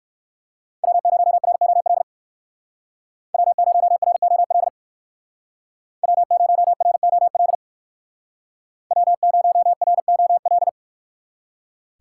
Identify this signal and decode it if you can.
Morse code, heard as a keyed tone, about 59 wpm, 700 Hz. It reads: W0ROL W0ROL W0ROL W0ROL